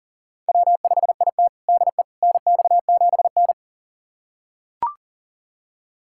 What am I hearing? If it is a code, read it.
Morse code, heard as a keyed tone, about 40 wpm, 700 Hz. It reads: W5IT DE NX7N
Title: USA Callsign DE Callsign - 4 Characters - 40wpm